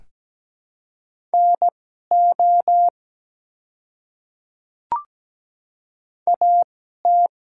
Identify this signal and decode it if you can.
Morse code, heard as a keyed tone, about 17 wpm, spaced as for 12 wpm, 700 Hz. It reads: NO AT
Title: 2x Character Spacing - Top 100 Words - 17wpm